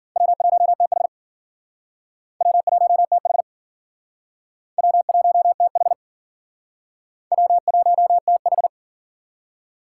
Morse code, heard as a keyed tone, about 52 wpm, 700 Hz. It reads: W1TH W1TH W1TH W1TH